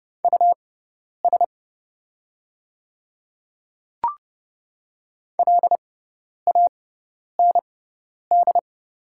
Morse code, heard as a keyed tone, about 30 wpm, spaced as for 10 wpm, 700 Hz. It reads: US LAND
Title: 6x Character Spacing - Top 300 Words - 30wpm